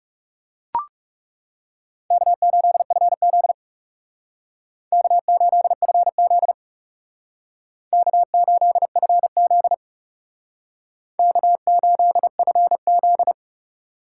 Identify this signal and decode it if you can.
Morse code, heard as a keyed tone, about 38 wpm, 700 Hz. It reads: K8FZ K8FZ K8FZ K8FZ